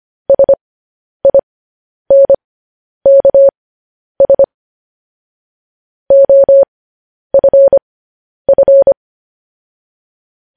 Morse code, heard as a keyed tone, about 25 wpm, spaced as for 10 wpm, 550 Hz. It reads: SINKS OFF